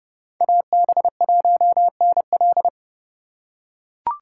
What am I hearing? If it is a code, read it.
Morse code, heard as a keyed tone, about 30 wpm, 700 Hz. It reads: AB1NL
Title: USA Call Signs - 5 Characters - 4x Character Spacing - 30wpm